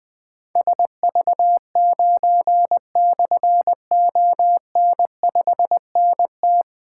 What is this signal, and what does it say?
Morse code, heard as a keyed tone, about 20 wpm, 700 Hz. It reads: SV9/ON5NT
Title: Difficult Call Signs 20wpm